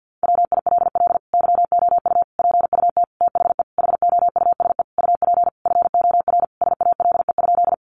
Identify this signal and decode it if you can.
Morse code, heard as a keyed tone, about 50 wpm, 700 Hz. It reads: WILL YOU PUT THE HOUSE UP FOR SALE?
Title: Sentences from Top 1400 Words 50wpm